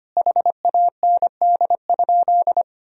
Morse code, heard as a keyed tone, about 25 wpm, 700 Hz. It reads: HAND?
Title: Sentences from Top 200 Words and Abbreviations 25wpm